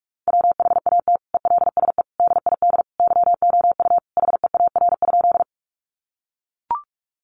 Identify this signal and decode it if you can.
Morse code, heard as a keyed tone, about 44 wpm, 700 Hz. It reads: WHAT ELSE DID YOU HEAR?
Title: Sentences from Top 800 Words 45wpm